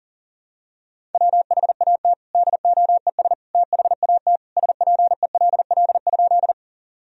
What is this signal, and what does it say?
Morse code, heard as a keyed tone, about 40 wpm, 700 Hz. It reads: WHAT DOES THAT SPELL?
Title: Sentences from Top 1000 Words - Review Part 1 - 40wpm